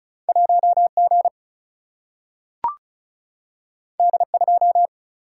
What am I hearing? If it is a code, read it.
Morse code, heard as a keyed tone, about 36 wpm, 700 Hz. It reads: 1G D2
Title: Single-Letter Single-Number 35wpm